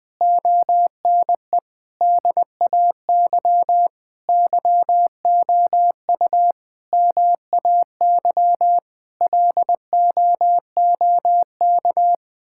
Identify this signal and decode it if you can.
Morse code, heard as a keyed tone, about 20 wpm, 700 Hz. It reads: ONE DAY YOU MAY LOOK
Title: Difficult Sentences with Hints - 20wpm